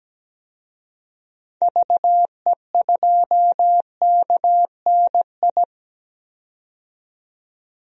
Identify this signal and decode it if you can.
Morse code, heard as a keyed tone, about 17 wpm, 700 Hz. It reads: VE2KNI
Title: DXCC-100 Call Signs - 6 Characters - 17wpm